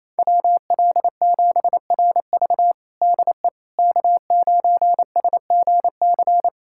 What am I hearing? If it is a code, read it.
Morse code, heard as a keyed tone, about 28 wpm, 700 Hz. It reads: WL7RV DE K9SGC
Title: USA Callsign DE Callsign - 5 Characters - 28wpm